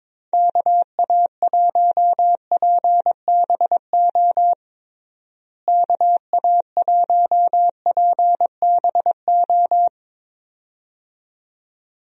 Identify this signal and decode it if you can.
Morse code, heard as a keyed tone, about 22 wpm, 700 Hz. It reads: KA1PBO KA1PBO
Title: USA Call Signs - 6 Characters - Repeated 2x - 22wpm